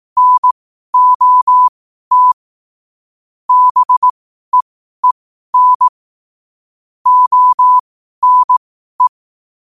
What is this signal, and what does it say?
Morse code, heard as a keyed tone, about 18 wpm, spaced as for 12 wpm, 1000 Hz. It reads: NOT BEEN ONE